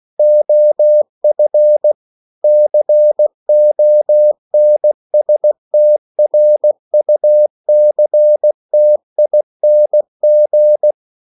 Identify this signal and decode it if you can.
Morse code, heard as a keyed tone, about 16 wpm, 600 Hz. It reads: OF CONSTRUCTING
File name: CW / Practice / EnglishQuotations0216